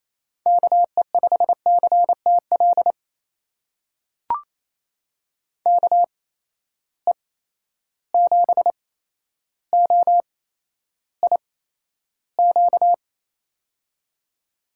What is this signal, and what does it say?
Morse code, heard as a keyed tone, about 28 wpm, spaced as for 12 wpm, 700 Hz. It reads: KE5CTL KE7OIQ
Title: USA Call Signs - 6 Characters - 8x Character Spacing - 28wpm